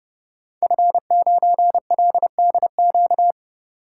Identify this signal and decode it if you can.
Morse code, heard as a keyed tone, about 30 wpm, 700 Hz. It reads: F9LDQ
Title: Code Groups - 5 Letter:Numbers - 30wpm